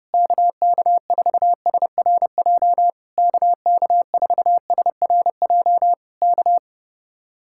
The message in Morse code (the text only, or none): KK4SRJ KK4SRJ K